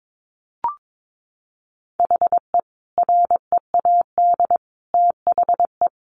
Morse code, heard as a keyed tone, about 22 wpm, 700 Hz. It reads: HE READ THE
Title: Sentences from Top 1100 Words 22wpm